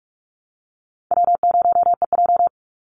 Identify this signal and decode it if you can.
Morse code, heard as a keyed tone, about 44 wpm, 700 Hz. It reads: W0EJ
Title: Easy Call Signs Repeated 3x - 45wpm